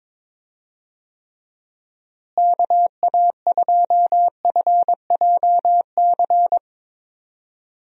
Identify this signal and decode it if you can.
Morse code, heard as a keyed tone, about 22 wpm, 700 Hz. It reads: KA2FJC
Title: USA Call Signs - 6 Characters - Repeated 2x - 22wpm